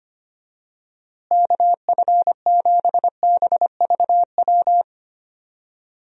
Morse code, heard as a keyed tone, about 25 wpm, 700 Hz. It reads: KF7BVW
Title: USA Call Signs - 6 Characters - Repeated 3x - 25wpm